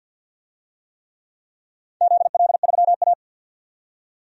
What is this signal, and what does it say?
Morse code, heard as a keyed tone, about 51 wpm, 700 Hz. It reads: ZL3A
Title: DXCC-100 Call Signs - 4 Characters - 50wpm